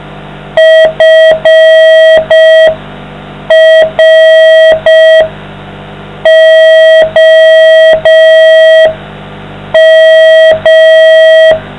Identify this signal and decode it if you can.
Morse code, heard as a keyed tone, about 5 wpm, 640 Hz, with noise. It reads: FROM